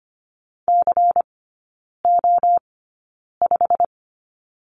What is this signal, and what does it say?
Morse code, heard as a keyed tone, about 25 wpm, spaced as for 9 wpm, 700 Hz. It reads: CO5
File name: course / 25:10wpm / 093 Lesson 16e - Letter c - Three Characters 25wpm